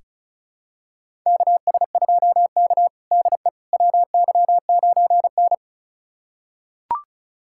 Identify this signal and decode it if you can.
Morse code, heard as a keyed tone, about 34 wpm, 700 Hz. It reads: KS2K DE WY9N
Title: USA Callsign DE Callsign - 4 Characters - 35wpm